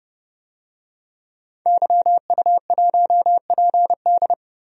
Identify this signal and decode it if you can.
Morse code, heard as a keyed tone, about 30 wpm, 700 Hz. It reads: YU1PD
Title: DXCC-100 Call Signs - 5 Characters - Repeated 2x - 30wpm